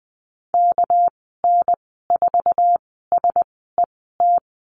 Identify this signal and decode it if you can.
Morse code, heard as a keyed tone, about 20 wpm, spaced as for 14 wpm, 700 Hz. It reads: KN4SET
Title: 200 Lesson 32g - Letter k - Call Signs 20wpm